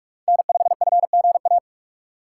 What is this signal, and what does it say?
Morse code, heard as a keyed tone, about 46 wpm, 700 Hz. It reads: N5FGA
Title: USA Call Signs - 5 Characters - 2x Character Spacing - 45wpm